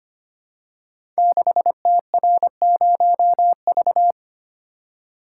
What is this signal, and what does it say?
Morse code, heard as a keyed tone, about 25 wpm, 700 Hz. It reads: 6TR0V
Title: Code Groups - 5 Letter:Numbers - 25wpm